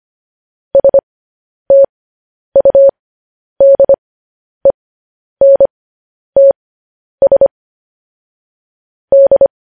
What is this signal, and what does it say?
Morse code, heard as a keyed tone, about 25 wpm, spaced as for 10 wpm, 550 Hz. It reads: STUDENTS D
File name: podcast / News headlines-1x-25x10